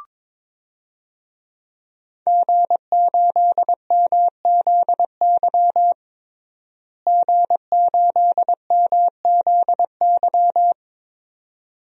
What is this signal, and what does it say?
Morse code, heard as a keyed tone, about 22 wpm, 700 Hz. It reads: G8MZY G8MZY
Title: DXCC-100 Call Signs - 5 Characters - Repeated 2x - 22wpm